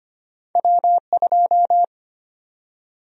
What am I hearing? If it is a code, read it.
Morse code, heard as a keyed tone, about 25 wpm, 700 Hz. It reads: W2
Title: USA Call Sign Prefixes - 25wpm